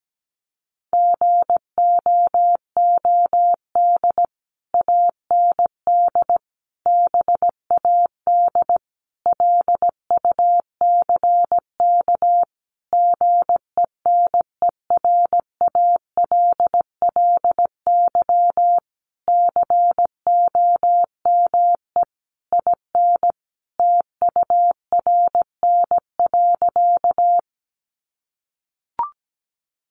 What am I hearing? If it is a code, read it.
Morse code, heard as a keyed tone, about 17 wpm, 700 Hz. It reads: GOOD AND BAD LUCK GENERALLY COME IN TURN.
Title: Sentences from Top 1500 Words 17wpm